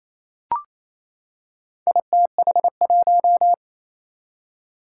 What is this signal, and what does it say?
Morse code, heard as a keyed tone, about 27 wpm, 700 Hz. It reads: ITH1
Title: Four Challenging Letter-Number Combinations 28wpm